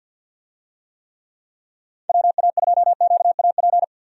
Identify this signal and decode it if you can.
Morse code, heard as a keyed tone, about 49 wpm, 700 Hz. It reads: WA2QAP
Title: USA Call Signs - 6 Characters - 50wpm